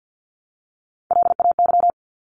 Morse code, heard as a keyed tone, about 50 wpm, 700 Hz. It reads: LAY
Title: Top 500 Words 50wpm